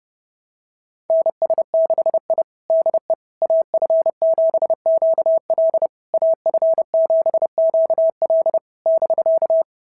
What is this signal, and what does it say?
Morse code, heard as a keyed tone, about 30 wpm, 690 Hz. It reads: NS6I DE AF7QL AF7QL <BK>
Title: SOTA QSO Practice 30wpm